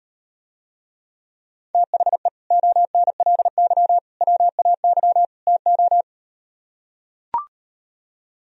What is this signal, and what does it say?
Morse code, heard as a keyed tone, about 38 wpm, 700 Hz. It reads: THE ONLY WAY TO